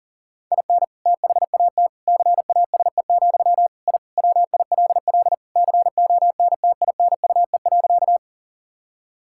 Morse code, heard as a keyed tone, about 40 wpm, 700 Hz. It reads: IN THAT CASE, I WILL CONTINUE.
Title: Sentences from Top 1000 Words - Review Part 6 - 40wpm